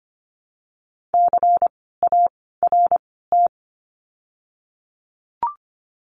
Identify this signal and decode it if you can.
Morse code, heard as a keyed tone, about 25 wpm, spaced as for 16 wpm, 700 Hz. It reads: CART
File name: course / 25:10wpm / 094 Lesson 16f - Letter c - Words 25wpm